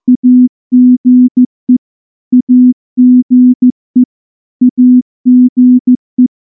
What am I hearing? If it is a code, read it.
Morse code, heard as a keyed tone, about 14 wpm, 250 Hz. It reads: AGE AGE AGE
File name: morse-training-15WPM